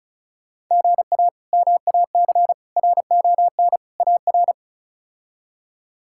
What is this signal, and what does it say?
Morse code, heard as a keyed tone, about 36 wpm, 700 Hz. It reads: GA MAC RON AR
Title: K1USN SST QSO Practice - 35wpm